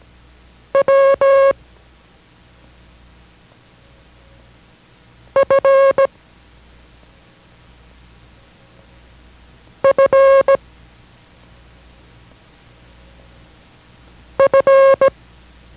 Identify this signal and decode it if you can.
Morse code, heard as a keyed tone, about 17 wpm, 540 Hz. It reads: W F F F